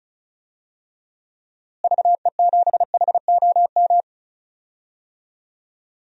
Morse code, heard as a keyed tone, about 36 wpm, 700 Hz. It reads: VE7HOM
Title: DXCC-100 Call Signs - 6 Characters - Repeated 3x - 35wpm